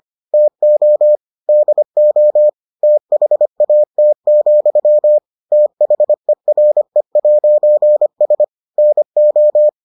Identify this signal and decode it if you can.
Morse code, heard as a keyed tone, about 25 wpm, 600 Hz. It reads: TO DO THAT, THERE'S NO